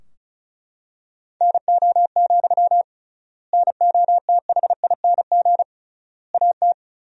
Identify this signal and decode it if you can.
Morse code, heard as a keyed tone, about 34 wpm, 700 Hz. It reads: NO, NOTHING AT
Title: 3x Word-Spacing - Sentences from Top 500 Words - 35wpm